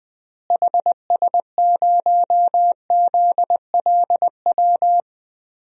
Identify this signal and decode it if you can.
Morse code, heard as a keyed tone, about 21 wpm, 700 Hz. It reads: HS0ZLW
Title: DXCC-100 Call Signs - 6 Characters - Repeated 2x - 20wpm